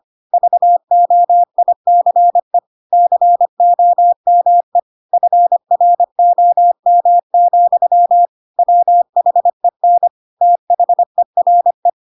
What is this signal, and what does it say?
Morse code, heard as a keyed tone, about 25 wpm, 700 Hz. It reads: VOICE COME FROM, WHEN THERE
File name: Pinocchio / Pinocchio ch1 25 25 ik1hge